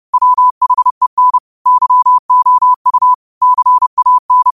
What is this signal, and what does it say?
Morse code, heard as a keyed tone, about 30 wpm, 1000 Hz. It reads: WHEN YOU CAN